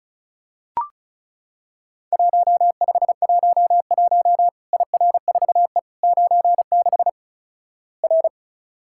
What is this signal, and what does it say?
Morse code, heard as a keyed tone, about 36 wpm, 700 Hz. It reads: 1511 IR4E 96 R
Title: Worked All Europe DX Contest - QTC Exchange - 35wpm